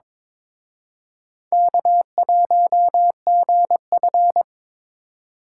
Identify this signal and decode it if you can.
Morse code, heard as a keyed tone, about 26 wpm, 700 Hz. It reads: K1GF
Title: Speed Racing - USA 1x2 Call Signs 22wpm